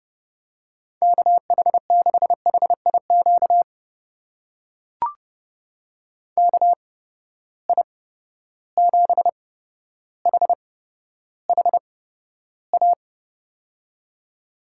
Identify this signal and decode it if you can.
Morse code, heard as a keyed tone, about 32 wpm, spaced as for 13 wpm, 700 Hz. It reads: KH6HIQ KI7HHA
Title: USA Call Signs - 6 Characters - 8x Character Spacing - 30wpm